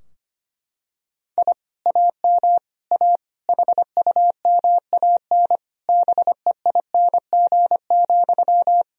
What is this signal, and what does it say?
Morse code, heard as a keyed tone, about 25 wpm, 700 Hz. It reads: I AM A HUMAN BEING,
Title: Sentences from Top 1000 Words - Review Part 3 - 25wpm